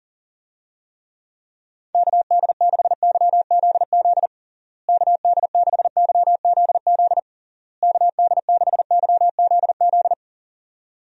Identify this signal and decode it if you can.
Morse code, heard as a keyed tone, about 40 wpm, 700 Hz. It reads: KD6YZZ KD6YZZ KD6YZZ
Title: USA Call Signs - 6 Characters - Repeated 3x - 40wpm